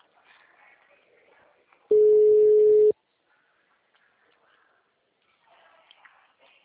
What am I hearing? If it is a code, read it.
Ringback tone (Europe, 425 Hz).